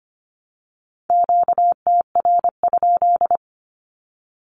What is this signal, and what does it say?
Morse code, heard as a keyed tone, about 25 wpm, 700 Hz. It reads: QTR?
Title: Q-Codes 25wpm